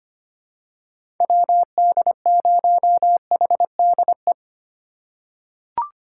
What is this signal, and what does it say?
Morse code, heard as a keyed tone, about 25 wpm, 700 Hz. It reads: WD0HDE
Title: USA Call Signs - 6 Characters - 4x Character Spacing - 25wpm